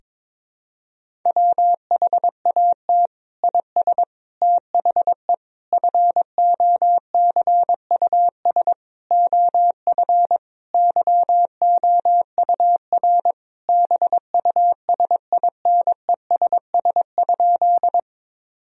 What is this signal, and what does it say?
Morse code, heard as a keyed tone, about 21 wpm, 700 Hz. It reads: WHAT IS THE FOCUS OF YOUR BUSINESS?
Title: Sentences from Top 1000 Words - Review Part 4 - 22wpm